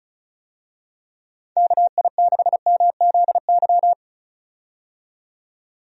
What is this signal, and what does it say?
Morse code, heard as a keyed tone, about 36 wpm, 700 Hz. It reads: KI6MZY
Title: USA Call Signs - 6 Characters - 35wpm